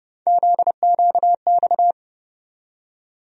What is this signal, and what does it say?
Morse code, heard as a keyed tone, about 30 wpm, 700 Hz. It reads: ZQX
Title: USA Call Sign Postfixes - 3 Letters 30wpm